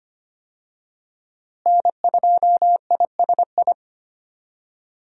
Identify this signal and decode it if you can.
Morse code, heard as a keyed tone, about 25 wpm, 700 Hz. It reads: N2ISI
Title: USA Call Signs - 5 Characters - Repeated 3x - 25wpm